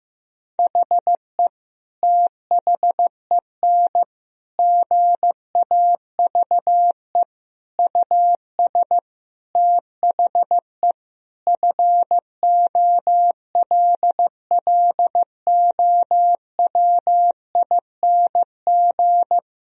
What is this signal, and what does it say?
Morse code, heard as a keyed tone, about 15 wpm, 700 Hz. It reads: HE THEN GAVE US THE FOLLOWING
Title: Sentences from Top 1000 Words - Review Part 4 - 15wpm